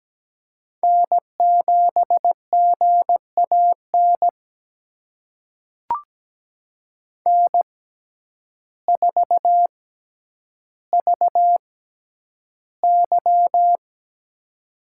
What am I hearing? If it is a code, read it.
Morse code, heard as a keyed tone, about 17 wpm, spaced as for 9 wpm, 700 Hz. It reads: N7GAN N4VY
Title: USA Call Signs - 6x Character Spacing - 17wpm